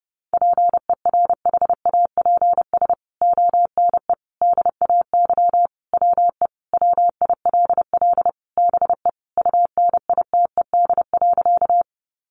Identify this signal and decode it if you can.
Morse code, heard as a keyed tone, about 28 wpm, 700 Hz. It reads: PERHAPS ONE DAY WE WILL BE UNITED.
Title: Sentences from Top 600 Words 30wpm